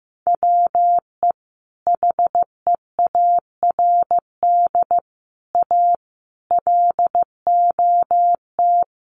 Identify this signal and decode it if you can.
Morse code, heard as a keyed tone, about 15 wpm, 700 Hz. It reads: WE HEARD A LOT
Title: Sentences from Top 1100 Words 15wpm